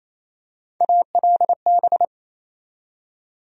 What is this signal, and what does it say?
Morse code, heard as a keyed tone, about 28 wpm, 700 Hz. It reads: ALB
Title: USA Call Sign Postfixes - 3 Letters 28wpm